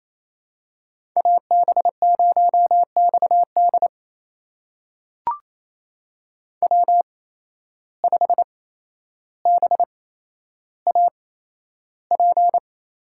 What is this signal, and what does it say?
Morse code, heard as a keyed tone, about 28 wpm, spaced as for 12 wpm, 700 Hz. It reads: AB0XD W5BAP
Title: USA Call Signs - 5 Characters - 8x Character Spacing - 28wpm